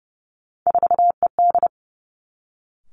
Morse code, heard as a keyed tone, about 30 wpm, 700 Hz. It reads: 4ED
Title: Three Challenging Letter-Number Combinations 30wpm